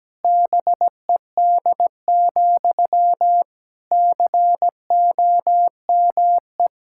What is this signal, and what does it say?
Morse code, heard as a keyed tone, about 17 wpm, 700 Hz. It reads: BED, COME
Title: Sentences from Top 1600 Words 17wpm